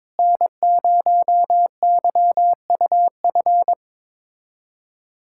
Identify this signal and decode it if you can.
Morse code, heard as a keyed tone, about 22 wpm, 700 Hz. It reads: N0YUF
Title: USA Callsign DE Callsign - 5 Characters - 22wpm